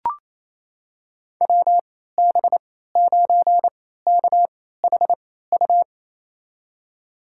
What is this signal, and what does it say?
Morse code, heard as a keyed tone, about 27 wpm, spaced as for 16 wpm, 700 Hz. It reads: WB9KHU
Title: USA Call Signs - 6 Characters - 3x Character Spacing - 28wpm